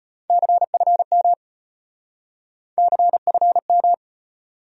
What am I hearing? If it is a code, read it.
Morse code, heard as a keyed tone, about 33 wpm, 700 Hz. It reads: CFM CFM